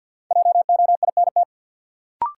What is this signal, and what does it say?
Morse code, heard as a keyed tone, about 52 wpm, 700 Hz. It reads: JOINT
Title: Sets of 5 Words Encore - New Word Drill - 50wpm